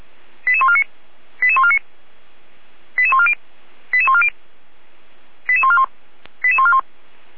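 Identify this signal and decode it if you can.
Five-tone selective call (ZVEI), heard as quick runs of five tones, tones of about 70 ms. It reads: 88149 88149 88140 88140 88141 88141
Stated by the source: ZVEI five-tone call 88194 received on an SDR, from the feuerwehr-orlamuende.org fire brigade audio server.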